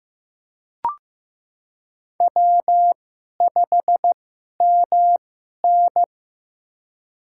Morse code, heard as a keyed tone, about 16 wpm, spaced as for 11 wpm, 700 Hz. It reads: W5MN
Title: USA Call Signs - 4 Characters - 2x Character Spacing - 15wpm